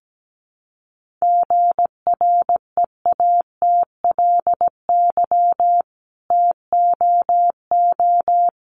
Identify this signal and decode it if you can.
Morse code, heard as a keyed tone, about 17 wpm, 700 Hz. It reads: GREATLY TOO
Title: Sentences from Top 1400 Words 17wpm